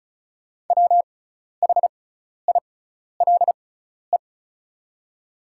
Morse code, heard as a keyed tone, about 36 wpm, spaced as for 12 wpm, 690 Hz. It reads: WHILE
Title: 6x Character Spacing - Top 500 Words - 35wpm